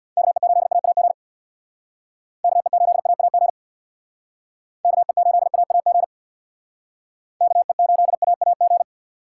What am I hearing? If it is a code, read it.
Morse code, heard as a keyed tone, about 62 wpm, 700 Hz. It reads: KE8AAG KE8AAG KE8AAG KE8AAG
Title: Speed Racing - USA Call Signs 50wpm